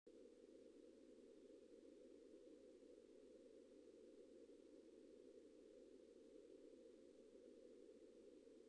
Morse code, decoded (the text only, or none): none